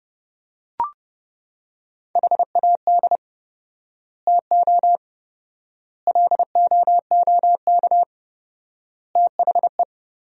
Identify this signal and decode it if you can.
Morse code, heard as a keyed tone, about 32 wpm, 700 Hz. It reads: HAD TO LOOK THE
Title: 4x Word-Spacing - Sentences from Top 100 Words - 30wpm